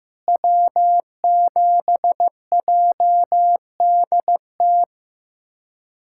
Morse code, heard as a keyed tone, about 15 wpm, 700 Hz. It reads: W7JDT
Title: USA Callsign DE Callsign - 5 Characters - 15wpm